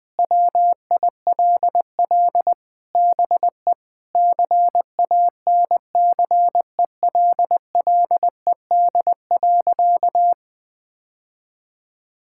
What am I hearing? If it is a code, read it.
Morse code, heard as a keyed tone, about 20 wpm, 700 Hz. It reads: WILL BE CANCELLED.
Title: Sentences with 8-Word Phrases - Spoken Beginning - 20wpm